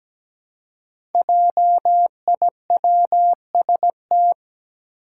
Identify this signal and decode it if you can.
Morse code, heard as a keyed tone, about 17 wpm, 700 Hz. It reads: JIWST
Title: Five Challenging Letter-Number Combinations 17wpm